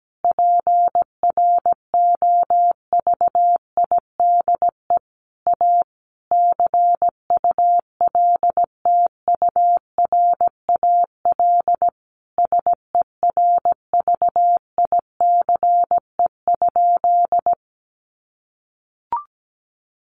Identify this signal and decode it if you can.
Morse code, heard as a keyed tone, about 17 wpm, 700 Hz. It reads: PROVIDE A CULTURAL SERVICE?
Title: Sentences from Top 1500 Words 17wpm